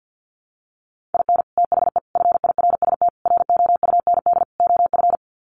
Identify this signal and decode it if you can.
Morse code, heard as a keyed tone, about 50 wpm, 700 Hz. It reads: IN THE FIRST ROUND OF
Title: Sets of 5 Words Encore 50wpm